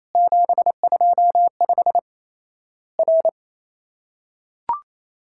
Morse code, heard as a keyed tone, about 28 wpm, 700 Hz. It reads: 725 R
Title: Worked All Europe DX Contest - QTC Exchange - 28wpm